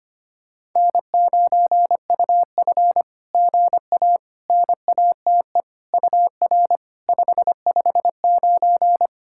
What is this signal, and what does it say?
Morse code, heard as a keyed tone, about 25 wpm, 660 Hz. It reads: N9UF GA NATE UR 559
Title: SOTA QSO Practice 25wpm